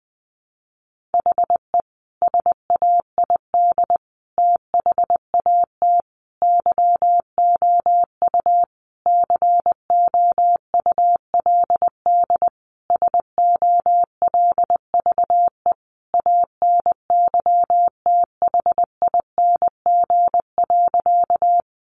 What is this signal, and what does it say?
Morse code, heard as a keyed tone, about 19 wpm, 700 Hz. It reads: HE SAID THAT YOU COULD SOLVE ANYTHING.
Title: Sentences from Top 1000 Words 20wpm